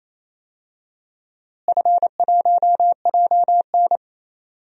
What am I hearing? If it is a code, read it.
Morse code, heard as a keyed tone, about 28 wpm, 700 Hz. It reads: F1JN
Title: DXCC-100 Call Signs - 4 Characters - Repeated 2x - 28wpm